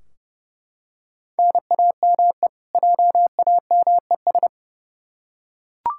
Morse code, heard as a keyed tone, about 31 wpm, 700 Hz. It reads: NAME JAMES
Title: Practice QSO - Exchange 1 - Beginning - 30wpm